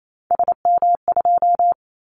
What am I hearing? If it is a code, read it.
Morse code, heard as a keyed tone, about 28 wpm, 700 Hz. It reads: SM2
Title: Callsign Prefixes for Top 25 DXCC - Repeated 3x - 28wpm